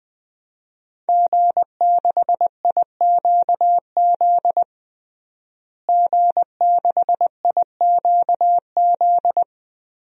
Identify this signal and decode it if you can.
Morse code, heard as a keyed tone, about 21 wpm, 700 Hz. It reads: G6IQZ G6IQZ
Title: DXCC-100 Call Signs - 5 Characters - Repeated 2x - 20wpm